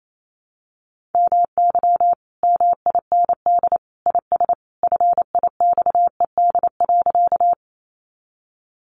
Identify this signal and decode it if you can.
Morse code, heard as a keyed tone, about 28 wpm, 700 Hz. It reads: MY MIND IS FIXED.
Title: Sentences from Top 1100 Words 28wpm